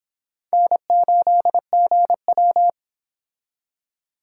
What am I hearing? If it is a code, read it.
Morse code, heard as a keyed tone, about 26 wpm, 700 Hz. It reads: N8GW